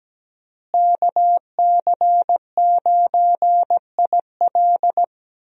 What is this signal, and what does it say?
Morse code, heard as a keyed tone, about 17 wpm, 700 Hz. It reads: KC9IL
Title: USA Call Signs - 5 Characters - Repeated 2x - 17wpm